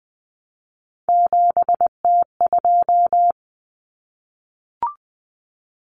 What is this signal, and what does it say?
Morse code, heard as a keyed tone, about 20 wpm, 700 Hz. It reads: 7T2
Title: Common Cut Numbers - 3-Digit 20wpm